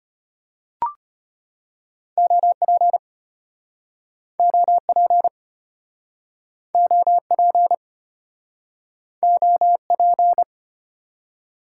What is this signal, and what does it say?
Morse code, heard as a keyed tone, about 34 wpm, 700 Hz. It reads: OP OP OP OP